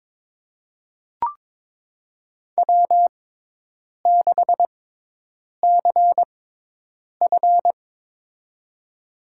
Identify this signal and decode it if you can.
Morse code, heard as a keyed tone, about 23 wpm, spaced as for 8 wpm, 700 Hz. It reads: W6CF
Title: USA Call Signs - 4 Characters - 6x Character Spacing - 22wpm